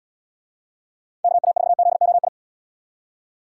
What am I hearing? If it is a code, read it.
Morse code, heard as a keyed tone, about 74 wpm, 700 Hz. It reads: KA4ZOI